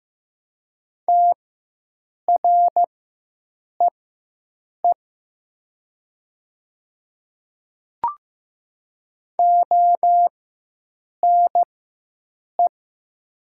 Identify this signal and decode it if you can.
Morse code, heard as a keyed tone, about 15 wpm, spaced as for 7 wpm, 700 Hz. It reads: TREE ONE
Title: 4x Character Spacing - Top 300 Words - 15wpm